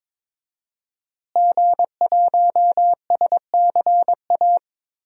Morse code, heard as a keyed tone, about 22 wpm, 700 Hz. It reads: G1SCA
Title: DXCC-100 Call Signs - 5 Characters - Repeated 3x - 22wpm